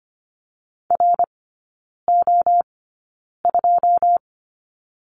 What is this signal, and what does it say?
Morse code, heard as a keyed tone, about 25 wpm, spaced as for 9 wpm, 700 Hz. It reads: RO2
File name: Lesson 17e - Letter u - Three Characters 25wpm